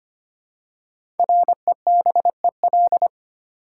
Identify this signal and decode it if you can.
Morse code, heard as a keyed tone, about 25 wpm, 700 Hz. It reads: REBEL
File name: 5 Letter Words - First 3 Letters Given - 25wpm